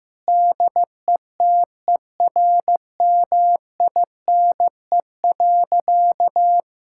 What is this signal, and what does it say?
Morse code, heard as a keyed tone, about 15 wpm, 700 Hz. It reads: DETERMINE.
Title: Sentences from Top 1000 Words - Review Part 5 - 15wpm